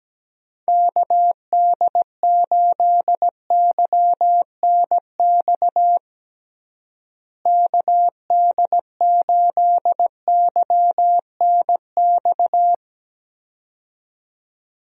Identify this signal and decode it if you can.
Morse code, heard as a keyed tone, about 17 wpm, 700 Hz. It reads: KD8YNX KD8YNX
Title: USA Call Signs - 6 Characters - Repeated 2x - 17wpm